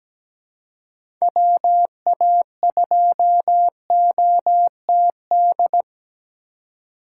Morse code, heard as a keyed tone, about 17 wpm, 700 Hz. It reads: WA2OTD
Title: USA Call Signs - 6 Characters - 4x Character Spacing - 17wpm